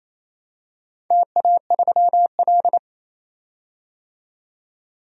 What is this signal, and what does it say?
Morse code, heard as a keyed tone, about 27 wpm, 700 Hz. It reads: TA3L
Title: DXCC-100 Call Signs - 4 Characters - 28wpm